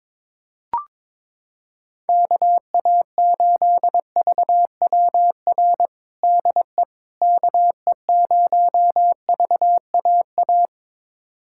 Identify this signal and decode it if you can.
Morse code, heard as a keyed tone, about 22 wpm, 700 Hz. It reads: KA8VWR DE KE0VAA
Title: USA Callsign DE Callsign - 6 Characters - 22wpm